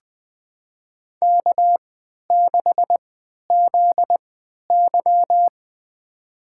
Morse code, heard as a keyed tone, about 20 wpm, spaced as for 11 wpm, 700 Hz. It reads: K6ZY
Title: USA Call Signs - 4 Characters - 3x Character Spacing - 20wpm